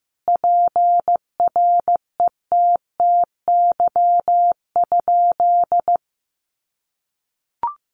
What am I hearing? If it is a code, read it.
Morse code, heard as a keyed tone, about 15 wpm, 700 Hz. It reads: PRETTY?
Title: Sentences from Top 1000 Words 15wpm